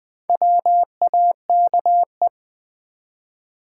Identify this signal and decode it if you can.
Morse code, heard as a keyed tone, about 20 wpm, 700 Hz. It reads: WAKE
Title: Build A Word - 2 Letters At A Time - 4-Letter Words - 20wpm